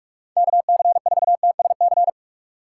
Morse code, heard as a keyed tone, about 47 wpm, 700 Hz. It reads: KC4TSC
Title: USA Call Signs - 6 Characters - 4x Character Spacing - 45wpm